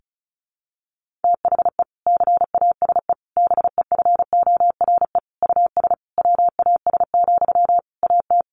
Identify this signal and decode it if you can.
Morse code, heard as a keyed tone, about 34 wpm, 700 Hz. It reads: THE CASE BEFORE US WAS, AT
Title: Sentences from Top 1400 Words 35wpm